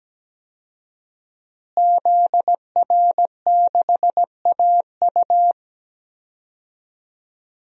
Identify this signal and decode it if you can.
Morse code, heard as a keyed tone, about 17 wpm, 700 Hz. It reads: ZR6AU
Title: DXCC-100 Call Signs - 5 Characters - Repeated 3x - 17wpm